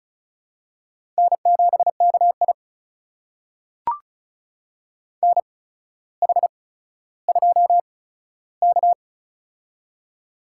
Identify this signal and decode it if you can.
Morse code, heard as a keyed tone, about 36 wpm, spaced as for 15 wpm, 700 Hz. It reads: N7KI NH2K
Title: USA Call Signs - 4 Characters - 8x Character Spacing - 35wpm